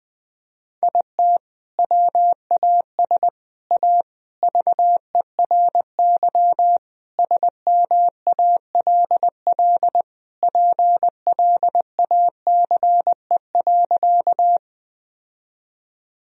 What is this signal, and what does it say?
Morse code, heard as a keyed tone, about 19 wpm, 700 Hz. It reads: IT WAS A VERY SMALL PLACE.
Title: Sentences from Top 1000 Words - Review Part 1 - 20wpm